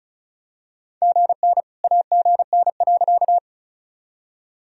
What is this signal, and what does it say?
Morse code, heard as a keyed tone, about 36 wpm, 700 Hz. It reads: GN AGN.
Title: Sentences from Top 300 Words and Abbreviations 35wpm